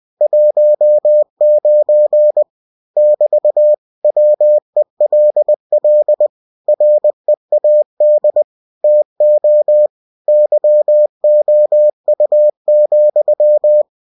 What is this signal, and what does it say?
Morse code, heard as a keyed tone, about 20 wpm, 600 Hz. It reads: 19 <BT> WELL READ TO YOU,
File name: cw-books / winnie-the-pooh / winnie-20wpm-page-0019